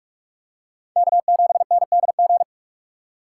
Morse code, heard as a keyed tone, about 43 wpm, 700 Hz. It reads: K7NDG